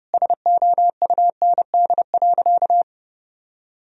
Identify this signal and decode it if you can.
Morse code, heard as a keyed tone, about 30 wpm, 700 Hz. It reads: SOUND.
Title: Sentences from Top 1000 Words - Review Part 1 - 30wpm